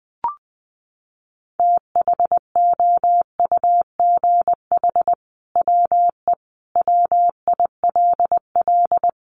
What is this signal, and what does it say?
Morse code, heard as a keyed tone, about 20 wpm, 700 Hz. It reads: THOUGH WE WILL
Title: Sentences from Top 1000 Words 20wpm